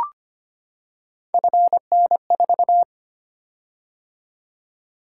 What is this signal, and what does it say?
Morse code, heard as a keyed tone, about 25 wpm, 700 Hz. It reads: FN4
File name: Three Letter-Number Combinations - 25wpm